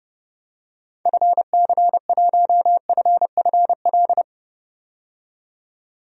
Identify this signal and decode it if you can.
Morse code, heard as a keyed tone, about 30 wpm, 700 Hz. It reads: FC1FFL
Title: DXCC-100 Call Signs - 6 Characters - 30wpm